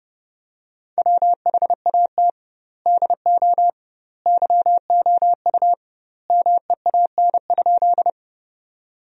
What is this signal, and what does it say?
Morse code, heard as a keyed tone, about 30 wpm, 700 Hz. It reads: WHAT DO YOU MEAN?
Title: 2x Word-Spacing - Sentences from Top 200 Words - 30wpm